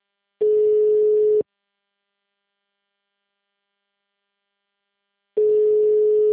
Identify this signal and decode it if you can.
Ringback tone (Europe, 425 Hz).